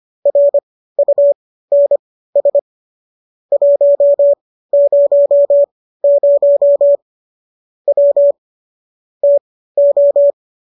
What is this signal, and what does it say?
Morse code, heard as a keyed tone, about 25 wpm, spaced as for 15 wpm, 570 Hz. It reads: RUNS 100 W TO